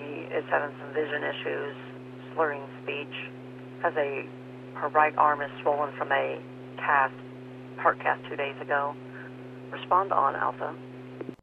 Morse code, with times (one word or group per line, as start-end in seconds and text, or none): none